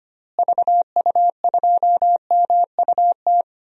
Morse code, heard as a keyed tone, about 25 wpm, 700 Hz. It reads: VU2MUT
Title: DXCC-100 Call Signs - 6 Characters - Repeated 2x - 25wpm